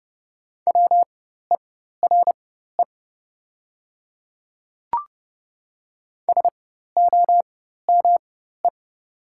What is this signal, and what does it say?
Morse code, heard as a keyed tone, about 30 wpm, spaced as for 14 wpm, 700 Hz. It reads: WERE SOME
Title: 4x Character Spacing - Top 100 Words - 30wpm